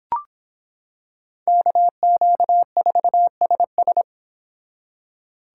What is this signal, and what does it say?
Morse code, heard as a keyed tone, about 26 wpm, 700 Hz. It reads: KQ4SS